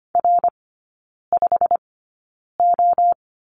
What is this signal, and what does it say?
Morse code, heard as a keyed tone, about 25 wpm, spaced as for 9 wpm, 700 Hz. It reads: R5O
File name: Lesson 15e - Number 5 - Three Characters 25wpm